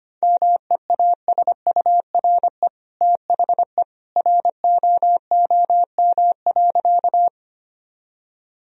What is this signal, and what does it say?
Morse code, heard as a keyed tone, about 25 wpm, 700 Hz. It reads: MEASURE THE ROOM.
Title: Sentences from Top 1000 Words - Review Part 2 - 25wpm